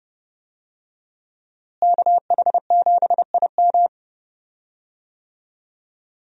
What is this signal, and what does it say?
Morse code, heard as a keyed tone, about 30 wpm, 700 Hz. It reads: KH7IM
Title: USA Call Signs - 5 Characters - Repeated 3x - 30wpm